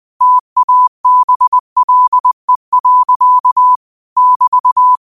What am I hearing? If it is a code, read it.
Morse code, heard as a keyed tone, about 20 wpm, 1000 Hz. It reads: TABLE. <BT>